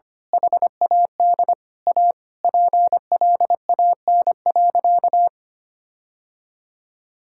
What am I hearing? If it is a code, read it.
Morse code, heard as a keyed tone, about 25 wpm, 700 Hz. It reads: HAD A PLAN.
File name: practice / 25wpm / Sentences with 2-Word Phrases - 25wpm